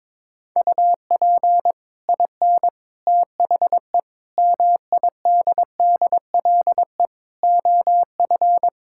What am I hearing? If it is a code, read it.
Morse code, heard as a keyed tone, about 21 wpm, 700 Hz. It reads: UP IN THE MIDDLE OF
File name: practice / 22wpm / Sets of 7 Words - 22wpm